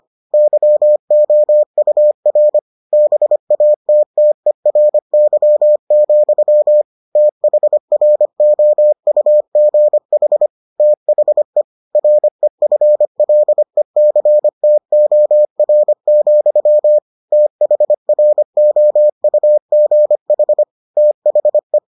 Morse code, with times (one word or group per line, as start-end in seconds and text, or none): 0.3-2.6 YOUR
2.9-6.8 BATTERY,
7.2-10.5 THROUGH
10.8-11.6 THE
11.9-17.0 REFLECTOR,
17.3-20.6 THROUGH
21.0-21.8 THE